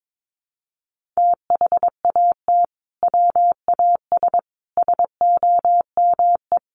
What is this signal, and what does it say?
Morse code, heard as a keyed tone, about 22 wpm, 700 Hz. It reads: THAT WAS SOME
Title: Sentences from Top 1400 Words 22wpm